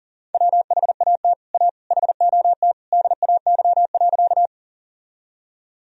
Morse code, heard as a keyed tone, about 40 wpm, 700 Hz. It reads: WHAT A HOT DAY.
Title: Sentences from Top 1000 Words - Review Part 3 - 40wpm